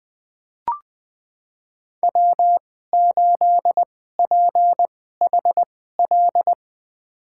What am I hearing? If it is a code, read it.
Morse code, heard as a keyed tone, about 21 wpm, spaced as for 15 wpm, 700 Hz. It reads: W8PHL
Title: USA Call Signs - 5 Characters - 2x Character Spacing - 20wpm